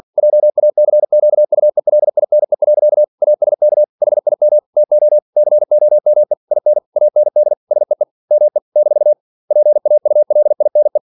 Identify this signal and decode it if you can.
Morse code, heard as a keyed tone, about 51 wpm, 600 Hz. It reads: JACQUELINE. ASK HIM TO COME IN AND SEE ME - PAULINE
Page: Never 47wpm